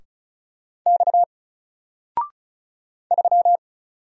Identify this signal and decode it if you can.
Morse code, heard as a keyed tone, about 36 wpm, 700 Hz. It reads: X 3
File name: Single-Letter-Number-35wpm